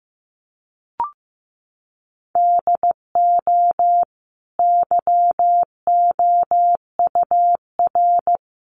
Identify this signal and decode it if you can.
Morse code, heard as a keyed tone, about 15 wpm, 700 Hz. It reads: DO YOUR
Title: Sentences from Top 1500 Words 15wpm